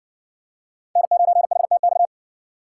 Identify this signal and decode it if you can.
Morse code, heard as a keyed tone, about 60 wpm, 700 Hz. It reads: N9HTX